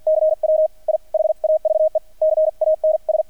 Morse code, heard as a keyed tone, about 48 wpm, 630 Hz. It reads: CW I HAVE CANS
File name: archived / CRSnet